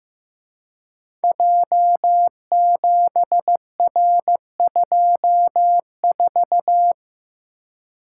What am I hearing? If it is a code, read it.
Morse code, heard as a keyed tone, about 15 wpm, 700 Hz. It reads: J7R24
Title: Code Groups - 5 Letter:Numbers - 15wpm